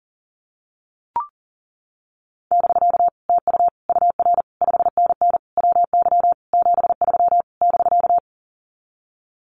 Morse code, heard as a keyed tone, about 40 wpm, 700 Hz. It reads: <BK> TU UR 5NN WY 73 <BK>
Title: Common QSO Phrases Encore 40wpm